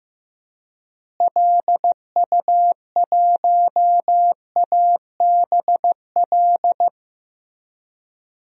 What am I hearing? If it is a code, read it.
Morse code, heard as a keyed tone, about 15 wpm, 700 Hz. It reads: LU1ABL
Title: DXCC-100 Call Signs - 6 Characters - Repeated 3x - 15wpm